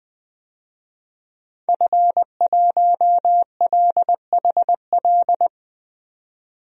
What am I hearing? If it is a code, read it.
Morse code, heard as a keyed tone, about 21 wpm, 700 Hz. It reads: F1LHL
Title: DXCC-100 Call Signs - 5 Characters - 20wpm